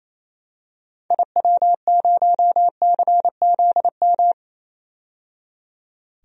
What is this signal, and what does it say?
Morse code, heard as a keyed tone, about 28 wpm, 700 Hz. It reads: IW0CZM
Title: DXCC-100 Call Signs - 6 Characters - 28wpm